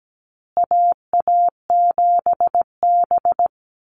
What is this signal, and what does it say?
Morse code, heard as a keyed tone, about 17 wpm, 700 Hz. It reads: AA7B
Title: USA Call Signs Repeated 2x 17wpm